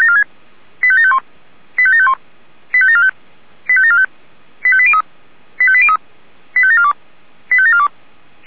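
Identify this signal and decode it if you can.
Five-tone selective call (ZVEI), heard as quick runs of five tones, tones of about 70 ms. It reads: 85751 85751 85745 85745 85803 85803 85742 85742
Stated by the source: ZVEI five-tone call 85942 received on SDR.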